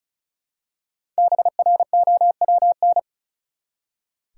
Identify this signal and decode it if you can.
Morse code, heard as a keyed tone, about 34 wpm, 700 Hz. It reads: BROWN
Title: Five Letter Words - 35wpm